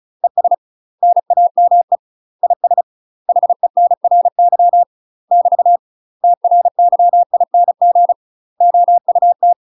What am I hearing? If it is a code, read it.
Morse code, heard as a keyed tone, about 34 wpm, 700 Hz. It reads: ES NAME IS HENRY <BT> TRYING OUT
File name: UK MASTER QSOs TRAINING / 35WPM / UK MASTER QSOs 0005